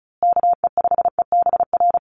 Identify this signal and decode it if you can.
Morse code, heard as a keyed tone, about 34 wpm, 700 Hz. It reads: KE5EBR
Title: USA Call Signs Repeated 2x 35wpm